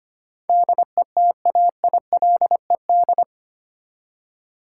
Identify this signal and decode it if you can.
Morse code, heard as a keyed tone, about 25 wpm, 700 Hz. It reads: DETAILED
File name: Eight Letter Words - 25wpm